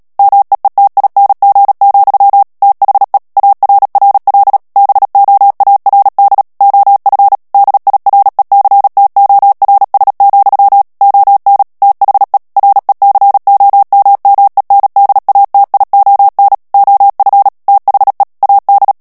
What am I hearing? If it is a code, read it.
Morse code, heard as a keyed tone, about 37 wpm, 800 Hz. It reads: MEETING, THE ARRL BOARD OF DIRECTORS, ON THE RECOMMENDATION OF THE AD
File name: morse mp3 / ANGLAIS texte clair 3 vitesse 35 mots minute